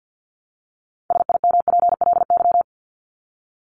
Morse code, heard as a keyed tone, about 50 wpm, 700 Hz. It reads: SIMPLY
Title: LY Suffix - Top 100 Words with Base Words 50wpm